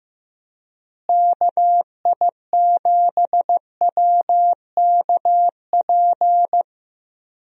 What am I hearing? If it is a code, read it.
Morse code, heard as a keyed tone, about 15 wpm, 700 Hz. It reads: KI7WKP
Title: USA Call Signs - 6 Characters - Repeated 2x - 15wpm